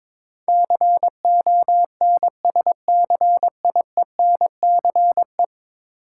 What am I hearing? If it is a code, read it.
Morse code, heard as a keyed tone, about 22 wpm, 700 Hz. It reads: CONSCIENCE
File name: practice / 22wpm / 10 Letter Words - First 3 Letters Given - 22wpm